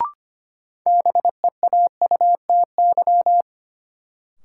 Morse code, heard as a keyed tone, about 25 wpm, 700 Hz. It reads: BEAUTY
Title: Sentences from Top 600 Words - New Word Drill - 25wpm